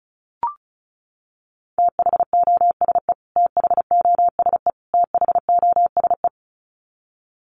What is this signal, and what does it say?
Morse code, heard as a keyed tone, about 34 wpm, 700 Hz. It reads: THOSE THOSE THOSE
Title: Top 500 Words Repeated 35wpm